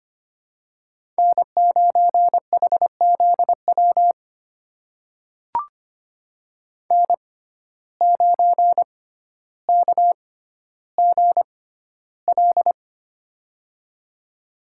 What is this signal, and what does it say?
Morse code, heard as a keyed tone, about 25 wpm, spaced as for 13 wpm, 700 Hz. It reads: N9HZW N9KGL
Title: USA Call Signs - 5 Characters - 6x Character Spacing - 25wpm